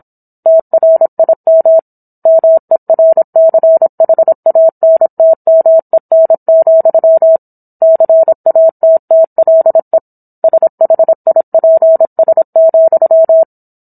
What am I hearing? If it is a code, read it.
Morse code, heard as a keyed tone, about 27 wpm, 650 Hz. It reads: TRIM MERCHANTMEN, CATTLE SHIPS,